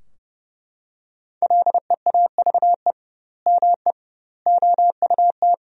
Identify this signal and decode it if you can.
Morse code, heard as a keyed tone, about 30 wpm, 700 Hz. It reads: LEAVE ME OUT
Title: 2x Word-Spacing - Sentences from Top 400 Words - 30wpm